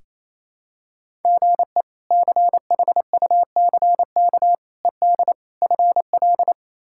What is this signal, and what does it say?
Morse code, heard as a keyed tone, about 28 wpm, 700 Hz. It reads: GE CHUCK ED FL
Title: K1USN SST QSO Practice - 28wpm